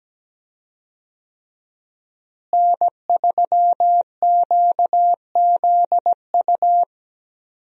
Morse code, heard as a keyed tone, about 17 wpm, 700 Hz. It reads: N3QZU
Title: USA Call Signs - 5 Characters - Repeated 3x - 17wpm